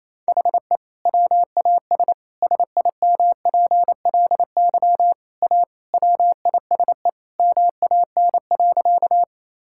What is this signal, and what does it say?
Morse code, heard as a keyed tone, about 28 wpm, 700 Hz. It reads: HE WAS SIMPLY A WISE MAN.
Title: Sentences from Top 1600 Words 28wpm